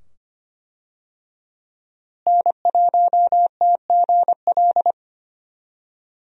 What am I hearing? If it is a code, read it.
Morse code, heard as a keyed tone, about 25 wpm, 700 Hz. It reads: N1TGL
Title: USA Call Signs - 5 Characters - Repeated 2x - 25wpm